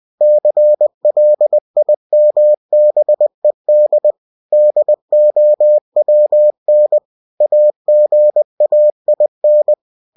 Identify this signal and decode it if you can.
Morse code, heard as a keyed tone, about 20 wpm, 600 Hz. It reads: CLIMBED DOWN AGAIN
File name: cw-books / winnie-the-pooh / winnie-20wpm-page-0094